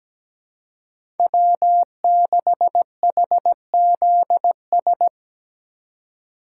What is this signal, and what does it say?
Morse code, heard as a keyed tone, about 17 wpm, 700 Hz. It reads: W6HZS
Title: USA Call Signs - 2x Character Spacing - 17wpm